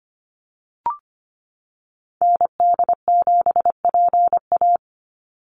Morse code, heard as a keyed tone, about 25 wpm, 690 Hz. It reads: ND7PA
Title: NA SOTA Activator Callsigns 25wpm